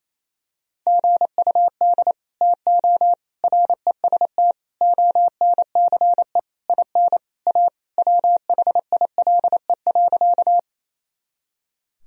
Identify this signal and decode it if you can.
Morse code, heard as a keyed tone, about 28 wpm, 700 Hz. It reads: GUD TO REST ONCE IN A WHILE.
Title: Sentences from Top 500 Words and Abbreviations 28wpm